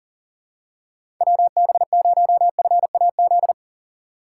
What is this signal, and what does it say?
Morse code, heard as a keyed tone, about 38 wpm, 700 Hz. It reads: WB0FAZ
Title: USA Call Signs - 6 Characters - 3x Character Spacing - 40wpm